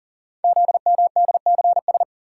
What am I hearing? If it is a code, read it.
Morse code, heard as a keyed tone, about 40 wpm, 700 Hz. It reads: ZMDCS
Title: Code Groups - 5 Letters - 40wpm